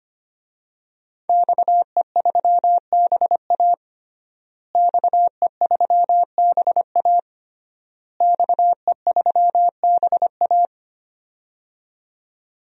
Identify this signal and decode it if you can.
Morse code, heard as a keyed tone, about 25 wpm, 700 Hz. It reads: XE3BA XE3BA XE3BA
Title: DXCC-100 Call Signs - 5 Characters - Repeated 3x - 25wpm